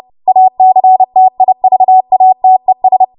Morse code, heard as a keyed tone, about 30 wpm, 750 Hz. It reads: ACTIVATES
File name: Morse / 30WPM